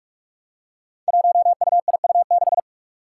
Morse code, heard as a keyed tone, about 46 wpm, 700 Hz. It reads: 1UIU6
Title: Five Challenging Letter-Number Combinations 45wpm